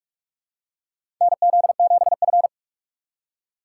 Morse code, heard as a keyed tone, about 46 wpm, 700 Hz. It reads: NZ7F
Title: USA Call Signs - 3x Character Spacing - 45wpm